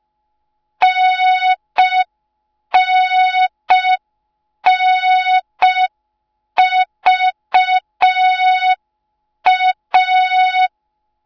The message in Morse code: NNNVA